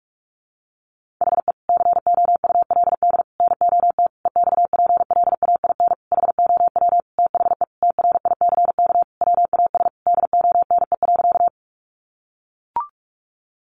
Morse code, heard as a keyed tone, about 44 wpm, 700 Hz. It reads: HE COULD NOT EXPLAIN HOW THE TRICK WAS DONE.
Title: Sentences from Top 1500 Words 45wpm